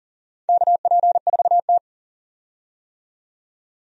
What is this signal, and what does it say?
Morse code, heard as a keyed tone, about 40 wpm, 700 Hz. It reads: KP4T
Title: DXCC-100 Call Signs - 4 Characters - 40wpm